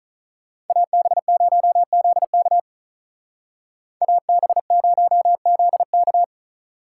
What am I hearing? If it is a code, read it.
Morse code, heard as a keyed tone, about 42 wpm, 700 Hz. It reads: AB0ZK AB0ZK
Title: Speed Racing - USA 2x2 Call Signs 35wpm